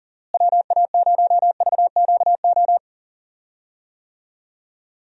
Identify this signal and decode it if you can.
Morse code, heard as a keyed tone, about 38 wpm, 700 Hz. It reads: WA0VQO
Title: USA Call Signs - 6 Characters - Repeated 3x - 40wpm